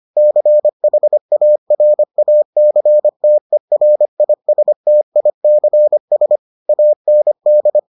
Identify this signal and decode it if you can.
Morse code, heard as a keyed tone, about 25 wpm, 600 Hz. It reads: CHARACTERISTICS AND